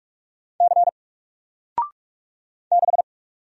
Morse code, heard as a keyed tone, about 48 wpm, 700 Hz. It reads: C 6
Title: Single Letter-Number Alt 45wpm